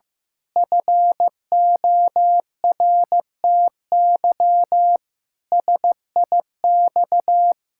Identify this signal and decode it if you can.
Morse code, heard as a keyed tone, about 15 wpm, 700 Hz. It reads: FORTY SIX
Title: Pangram Challenge v2 - 15wpm